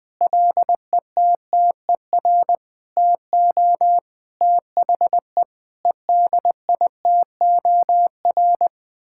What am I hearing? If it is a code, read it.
Morse code, heard as a keyed tone, about 20 wpm, 700 Hz. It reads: LETTER TO THE EDITOR
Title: Sets of 6 Words - 20wpm